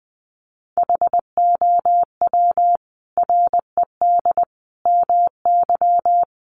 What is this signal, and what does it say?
Morse code, heard as a keyed tone, about 20 wpm, 700 Hz. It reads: HOW RED MY
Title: Sentences from Top 1100 Words 20wpm